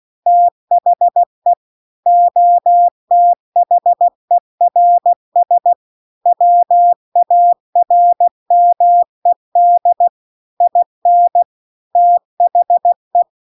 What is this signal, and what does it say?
Morse code, heard as a keyed tone, about 16 wpm, 700 Hz. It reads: THE OTHERS WARMED IN THE